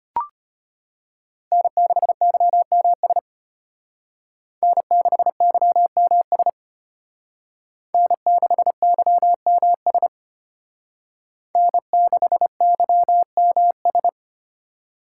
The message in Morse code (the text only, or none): N6YMS N6YMS N6YMS N6YMS